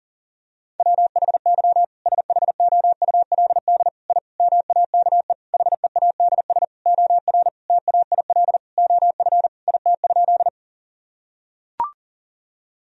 Morse code, heard as a keyed tone, about 40 wpm, 700 Hz. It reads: WHY SHOULD I MAKE HEADS OR TAIL OF IT?
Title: Sentences from Top 1000 Words - Review Part 3 - 40wpm